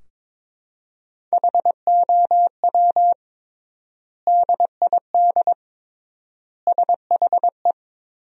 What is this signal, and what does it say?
Morse code, heard as a keyed tone, about 22 wpm, 700 Hz. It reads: HOW DID SHE
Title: 3x Word-Spacing - Sentences from Top 100 Words - 22wpm